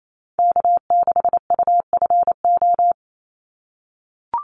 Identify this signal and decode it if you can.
Morse code, heard as a keyed tone, about 28 wpm, 700 Hz. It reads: K6UFO
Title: Call Signs - 28wpm